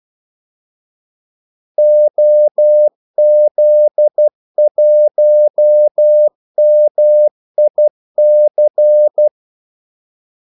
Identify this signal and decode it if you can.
Morse code, heard as a keyed tone, about 12 wpm, 600 Hz. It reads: OZ1MIC
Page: 2 ekstra ordmellemrum mellem kaldesignalerne